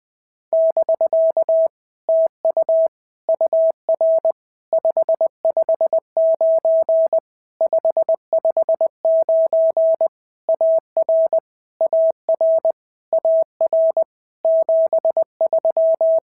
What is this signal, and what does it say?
Morse code, heard as a keyed tone, about 20 wpm, 650 Hz. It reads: <BK> TU UR 559 559 AR AR AR 73
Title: POTA QSO Practice - 20wpm